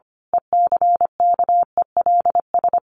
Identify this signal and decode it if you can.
Morse code, heard as a keyed tone, about 25 wpm, 700 Hz. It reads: ECKELS
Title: A Sound Of Thunder 25wpm